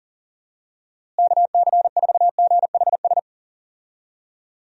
Morse code, heard as a keyed tone, about 40 wpm, 700 Hz. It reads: KC4GHS
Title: USA Call Signs - 6 Characters - Repeated 2x - 40wpm